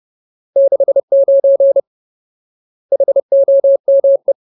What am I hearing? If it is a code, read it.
Morse code, heard as a keyed tone, about 30 wpm, 550 Hz. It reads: 69 HOME
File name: Ultimate Word List / 30WPM Ultimate